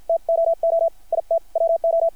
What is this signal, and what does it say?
Morse code, heard as a keyed tone, about 50 wpm, spaced as for 44 wpm, 660 Hz. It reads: TOO IT WO
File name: archived / CRSnet